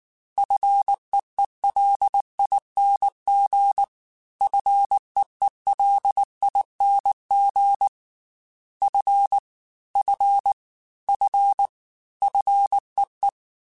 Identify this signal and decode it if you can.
Morse code, heard as a keyed tone, about 19 wpm, 800 Hz. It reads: FEELING FEELING F F F FEE
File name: F and L / F and L mp3 / F and L Prefixes 3x2 Twice @18